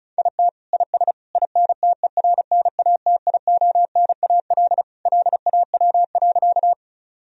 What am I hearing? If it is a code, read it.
Morse code, heard as a keyed tone, about 35 wpm, 700 Hz. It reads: IT IS INTERNATIONAL LAW.
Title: Sentences from Top 1000 Words - Review Part 4 - 35wpm